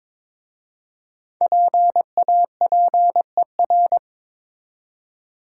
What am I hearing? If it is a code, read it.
Morse code, heard as a keyed tone, about 22 wpm, 700 Hz. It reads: PAPER
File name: 5 Letter Words - First 3 Letters Given - 22wpm